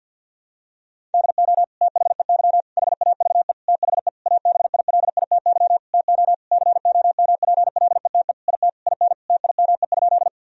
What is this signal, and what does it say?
Morse code, heard as a keyed tone, about 52 wpm, 700 Hz. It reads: DO THEY HAVE THE ABILITY TO COMPLETE IT IN TIME?
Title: Sentences from Top 1000 Words - Review Part 3 - 50wpm